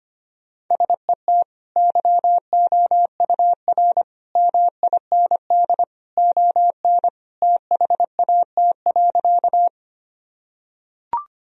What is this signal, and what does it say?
Morse code, heard as a keyed tone, about 25 wpm, 700 Hz. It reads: SET YOUR MIND ON THAT.
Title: Sentences from Top 1000 Words - Review Part 1 - 25wpm